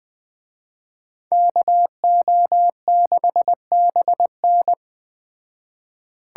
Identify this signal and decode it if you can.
Morse code, heard as a keyed tone, about 20 wpm, 700 Hz. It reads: KO6BN
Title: Speed Racing - USA 2x2 Call Signs 17wpm